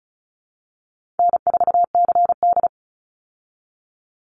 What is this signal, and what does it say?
Morse code, heard as a keyed tone, about 34 wpm, 700 Hz. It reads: N4CD
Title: Call Signs Repeated 3x - 35wpm